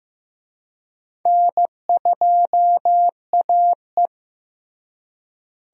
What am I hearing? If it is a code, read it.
Morse code, heard as a keyed tone, about 15 wpm, 700 Hz. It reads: N2AE
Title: USA Call Signs - 4 Characters - Repeated 3x - 15wpm